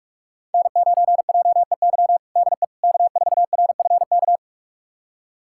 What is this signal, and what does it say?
Morse code, heard as a keyed tone, about 46 wpm, 700 Hz. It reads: N9JEY DE K4RFK
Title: USA Callsign DE Callsign - 5 Characters - 45wpm